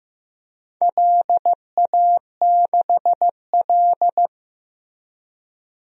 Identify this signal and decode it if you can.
Morse code, heard as a keyed tone, about 15 wpm, 700 Hz. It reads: LA6L
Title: DXCC-100 Call Signs - 4 Characters - Repeated 3x - 15wpm